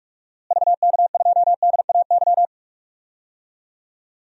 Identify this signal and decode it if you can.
Morse code, heard as a keyed tone, about 45 wpm, 700 Hz. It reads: VK2DAY
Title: DXCC-100 Call Signs - 6 Characters - Repeated 3x - 45wpm